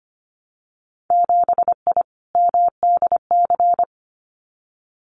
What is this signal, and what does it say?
Morse code, heard as a keyed tone, about 25 wpm, 700 Hz. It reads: 7I MDC
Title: Winter Field Day Exchanges Encore 25wpm